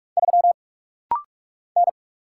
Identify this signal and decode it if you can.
Morse code, heard as a keyed tone, about 44 wpm, 700 Hz. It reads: 3 N
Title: Single Letter-Number Alt - Rapid-Fire 45wpm